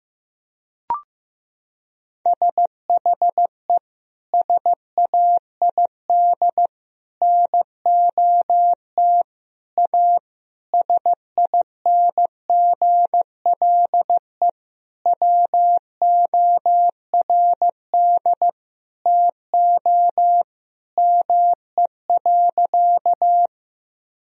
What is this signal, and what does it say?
Morse code, heard as a keyed tone, about 15 wpm, 700 Hz. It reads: SHE SAID NOT A SINGLE WORD TO ME.
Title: Sentences from Top 1000 Words - Review Part 4 - 15wpm